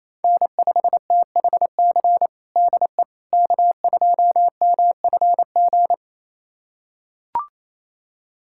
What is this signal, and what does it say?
Morse code, heard as a keyed tone, about 28 wpm, 700 Hz. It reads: N5THC DE K2MFG
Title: USA Callsign DE Callsign - 5 Characters - 28wpm